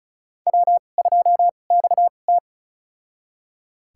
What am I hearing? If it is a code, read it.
Morse code, heard as a keyed tone, about 34 wpm, spaced as for 25 wpm, 700 Hz. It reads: W2XT
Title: USA Call Signs - 2x Character Spacing - 35wpm